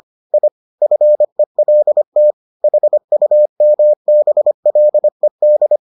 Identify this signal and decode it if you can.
Morse code, heard as a keyed tone, about 25 wpm, 600 Hz. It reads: I FELT HUMBLED